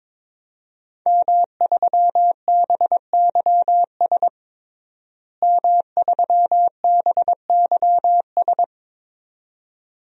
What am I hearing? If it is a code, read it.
Morse code, heard as a keyed tone, about 21 wpm, 700 Hz. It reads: M3BYS M3BYS
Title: DXCC-100 Call Signs - 5 Characters - Repeated 2x - 22wpm